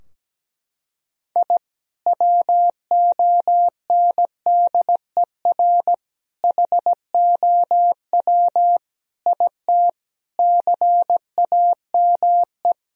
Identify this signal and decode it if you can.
Morse code, heard as a keyed tone, about 17 wpm, 700 Hz. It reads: I WONDER HOW IT CAME
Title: Sentences from Top 1000 Words - Review Part 3 - 17wpm